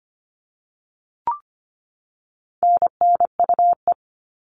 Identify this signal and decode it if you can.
Morse code, heard as a keyed tone, about 25 wpm, 700 Hz. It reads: NNUE
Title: All Cut Numbers - 4-Digit 25wpm